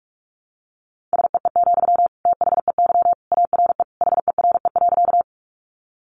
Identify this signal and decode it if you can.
Morse code, heard as a keyed tone, about 44 wpm, 700 Hz. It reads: SEE, THEY ARE HERE.
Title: Sentences from Top 200 Words 45wpm